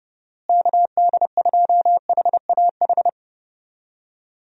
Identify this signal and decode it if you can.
Morse code, heard as a keyed tone, about 30 wpm, 700 Hz. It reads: KD2HAH
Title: USA Call Signs - 6 Characters - Repeated 3x - 30wpm